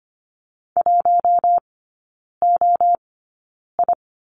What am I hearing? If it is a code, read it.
Morse code, heard as a keyed tone, about 25 wpm, spaced as for 9 wpm, 690 Hz. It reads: 1OI
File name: Lesson 15e - Number 5 - Three Characters 25wpm